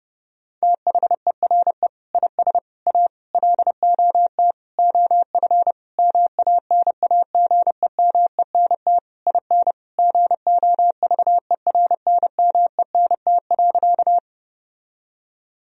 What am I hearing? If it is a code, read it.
Morse code, heard as a keyed tone, about 30 wpm, 700 Hz. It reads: THERE IS A LOT OF MANAGEMENT IN GOVERNMENT.
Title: Sentences from Top 1000 Words - Review Part 4 - 30wpm